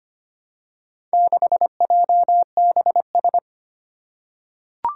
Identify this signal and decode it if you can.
Morse code, heard as a keyed tone, about 25 wpm, 700 Hz. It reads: 6JBS
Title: Four Challenging Letter-Number Combinations 25wpm